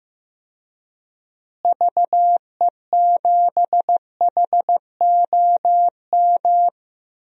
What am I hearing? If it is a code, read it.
Morse code, heard as a keyed tone, about 15 wpm, 700 Hz. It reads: VE7HOM
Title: DXCC-100 Call Signs - 6 Characters - Repeated 3x - 15wpm